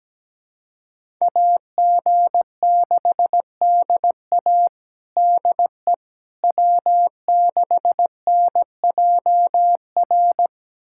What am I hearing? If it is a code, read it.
Morse code, heard as a keyed tone, about 17 wpm, 700 Hz. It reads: AG6DA DE W6NJR
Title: USA Callsign DE Callsign - 5 Characters - 17wpm